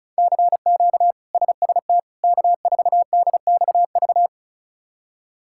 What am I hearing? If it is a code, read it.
Morse code, heard as a keyed tone, about 36 wpm, 700 Hz. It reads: CQ SST K4DXV
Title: K1USN SST QSO Practice - 35wpm